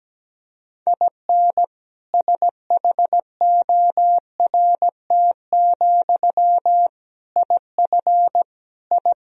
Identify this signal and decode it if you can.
Morse code, heard as a keyed tone, about 17 wpm, 700 Hz. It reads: IN SHORT, IF I
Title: Sentences from Top 1000 Words - Review Part 2 - 17wpm